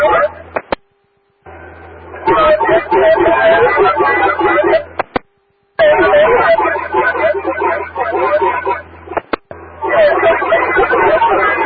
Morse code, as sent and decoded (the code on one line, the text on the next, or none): none